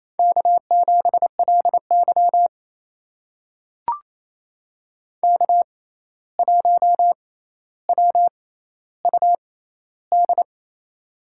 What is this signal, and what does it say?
Morse code, heard as a keyed tone, about 28 wpm, spaced as for 13 wpm, 700 Hz. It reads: K7LY K1WUD
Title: USA Call Signs - 6x Character Spacing - 28wpm